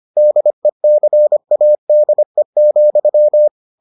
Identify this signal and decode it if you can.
Morse code, heard as a keyed tone, about 25 wpm, 600 Hz. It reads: DECADE,